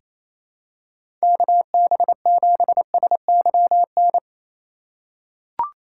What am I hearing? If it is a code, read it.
Morse code, heard as a keyed tone, about 28 wpm, 700 Hz. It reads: KB7SYN
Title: USA Call Signs - 6 Characters - 3x Character Spacing - 28wpm